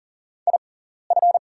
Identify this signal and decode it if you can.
Morse code, heard as a keyed tone, about 40 wpm, spaced as for 14 wpm, 700 Hz. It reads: IF
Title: 6x Character Spacing - Top 200 Words - 40wpm